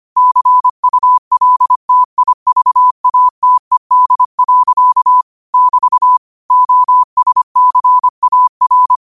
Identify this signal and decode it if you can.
Morse code, heard as a keyed tone, about 25 wpm, 1000 Hz. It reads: CULTIVATED. <BT> OSCAR
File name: qotdfiles / 25wpm